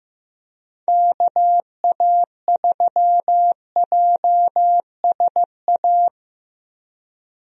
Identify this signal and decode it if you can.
Morse code, heard as a keyed tone, about 15 wpm, 700 Hz. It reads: KA3JSA
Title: USA Call Signs - 6 Characters - Repeated 3x - 15wpm